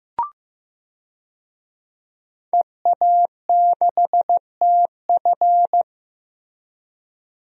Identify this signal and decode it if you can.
Morse code, heard as a keyed tone, about 15 wpm, 700 Hz. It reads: EA6TF
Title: DXCC-100 Call Signs - 5 Characters - Repeated 3x - 15wpm